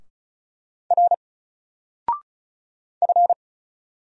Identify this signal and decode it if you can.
Morse code, heard as a keyed tone, about 36 wpm, 700 Hz. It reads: R F
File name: Single-Letter-Number-35wpm